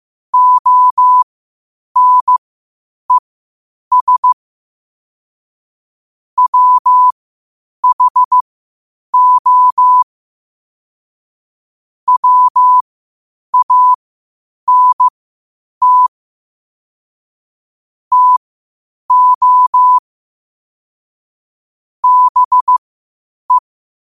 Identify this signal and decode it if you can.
Morse code, heard as a keyed tone, about 15 wpm, spaced as for 8 wpm, 1000 Hz. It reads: ONES WHO WANT TO BE
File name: qotdfiles / 8wpm